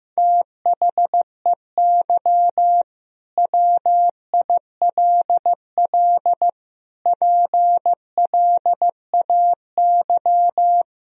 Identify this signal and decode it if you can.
Morse code, heard as a keyed tone, about 16 wpm, 700 Hz. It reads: THEY WILL PLAY
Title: Sentences from Top 1000 Words - Review Part 4 - 15wpm